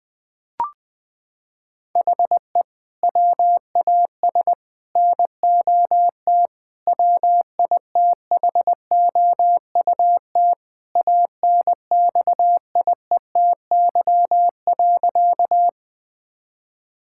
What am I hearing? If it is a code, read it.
Morse code, heard as a keyed tone, about 21 wpm, 700 Hz. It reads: HE WAS NOT WITHOUT ANXIETY.
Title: Sentences from Top 1600 Words 20wpm